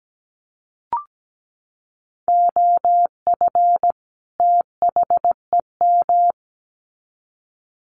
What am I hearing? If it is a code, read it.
Morse code, heard as a keyed tone, about 17 wpm, 700 Hz. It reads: OF THEM
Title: Sets of 2 Words 17wpm